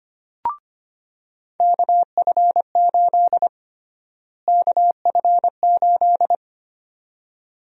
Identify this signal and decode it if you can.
Morse code, heard as a keyed tone, about 25 wpm, 700 Hz. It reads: KF8 KF8
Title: USA Call Sign Prefixes - Repeated 2x - 25wpm